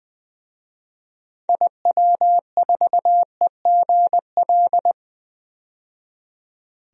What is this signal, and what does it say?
Morse code, heard as a keyed tone, about 20 wpm, 700 Hz. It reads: IW4EGL
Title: DXCC-100 Call Signs - 6 Characters - 20wpm